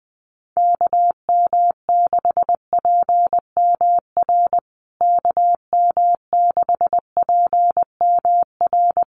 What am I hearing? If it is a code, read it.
Morse code, heard as a keyed tone, about 19 wpm, 700 Hz. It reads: KM6PMR KM6PMR
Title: USA Call Signs Repeated 2x 20wpm